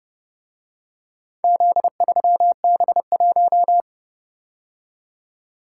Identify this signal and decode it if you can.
Morse code, heard as a keyed tone, about 30 wpm, 700 Hz. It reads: Z3B1